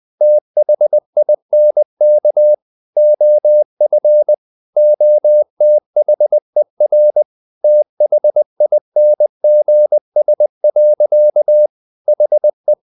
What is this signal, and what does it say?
Morse code, heard as a keyed tone, about 20 wpm, 600 Hz. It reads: THINK OF OTHER THINGS. HE
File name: the old man and the sea / the old man and the sea part 1 20wpm